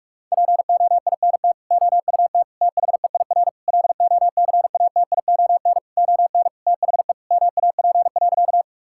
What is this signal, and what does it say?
Morse code, heard as a keyed tone, about 46 wpm, 700 Hz. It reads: POINT OUT THEIR LOCATION ON THE MAP.
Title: Sentences from Top 1000 Words - Review Part 6 - 45wpm